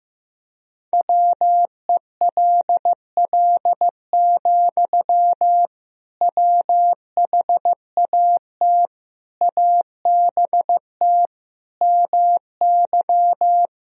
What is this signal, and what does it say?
Morse code, heard as a keyed tone, about 15 wpm, 700 Hz. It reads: WELL, WHAT ABT MY
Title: Sentences from Top 500 Words and Abbreviations 15wpm